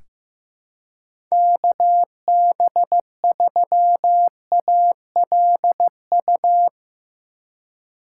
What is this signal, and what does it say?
Morse code, heard as a keyed tone, about 15 wpm, 700 Hz. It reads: KB3ALU
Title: USA Call Signs - 6 Characters - 3x Character Spacing - 15wpm